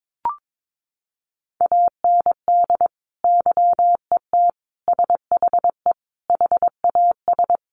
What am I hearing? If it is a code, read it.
Morse code, heard as a keyed tone, about 22 wpm, 700 Hz. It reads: AND YET SHE HAS
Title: Sentences from Top 1100 Words 22wpm